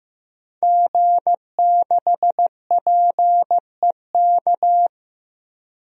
Morse code, heard as a keyed tone, about 15 wpm, 700 Hz. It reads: G6PEK
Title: DXCC-100 Call Signs - 5 Characters - 15wpm